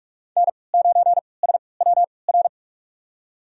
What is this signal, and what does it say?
Morse code, heard as a keyed tone, about 41 wpm, spaced as for 25 wpm, 700 Hz. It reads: N9SWR
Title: USA Call Signs - 5 Characters - 3x Character Spacing - 45wpm